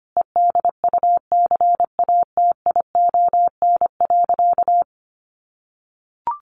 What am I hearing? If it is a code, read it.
Morse code, heard as a keyed tone, about 25 wpm, 700 Hz. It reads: EDUCATION.
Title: Sentences from Top 1500 Words 25wpm